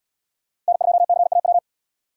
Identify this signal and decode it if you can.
Morse code, heard as a keyed tone, about 71 wpm, 700 Hz. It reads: N2CAW